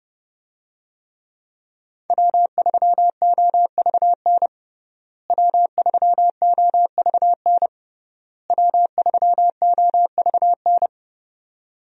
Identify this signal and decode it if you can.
Morse code, heard as a keyed tone, about 32 wpm, 700 Hz. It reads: W3OVN W3OVN W3OVN
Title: USA Call Signs - 5 Characters - Repeated 3x - 30wpm